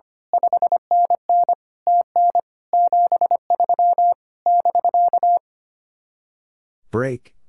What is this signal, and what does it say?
Morse code, heard as a keyed tone, about 24 wpm, 700 Hz. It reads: 5NN TN 73 <BK>
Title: Speed-Racing Common QSO Phrases Encore - 25wpm